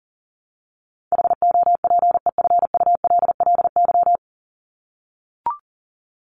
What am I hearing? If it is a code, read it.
Morse code, heard as a keyed tone, about 40 wpm, 700 Hz. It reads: HOPEFULLY
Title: Top 1251-1500 Words 40wpm